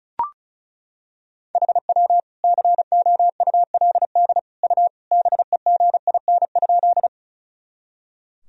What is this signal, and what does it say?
Morse code, heard as a keyed tone, about 35 wpm, 700 Hz. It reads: HW COULD U BEGIN?
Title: Sentences from Top 300 Words and Abbreviations 35wpm